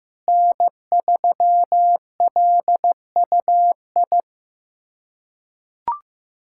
Morse code, heard as a keyed tone, about 15 wpm, 700 Hz. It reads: N3LUI
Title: USA Callsign DE Callsign - 5 Characters - 15wpm